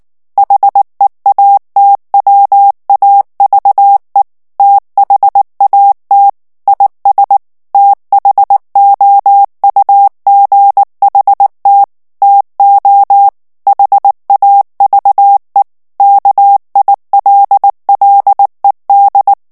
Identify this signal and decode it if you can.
Morse code, heard as a keyed tone, about 19 wpm, 800 Hz. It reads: HEATWAVE THAT IS THOUGHT TO HAVE KILLED
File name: ANGLAIS texte clair 2 vitesse 18 mots minute